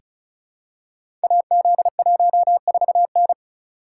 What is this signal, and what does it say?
Morse code, heard as a keyed tone, about 36 wpm, 690 Hz. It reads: AZ14N
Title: Code Groups - 5 Letter:Numbers - 35wpm